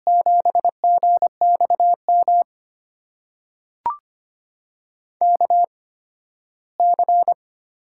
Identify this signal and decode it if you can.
Morse code, heard as a keyed tone, about 25 wpm, spaced as for 15 wpm, 700 Hz. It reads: 7GXM KC
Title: USA Call Signs - 6 Characters - 8x Character Spacing - 25wpm